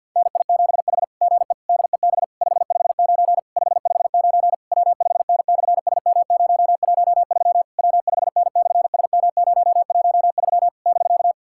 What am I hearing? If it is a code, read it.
Morse code, heard as a keyed tone, about 50 wpm, 690 Hz. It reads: NI7H GE DEB 559 559 W5N/SM013 W5N/SM013 <BK>
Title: SOTA QSO Practice 50wpm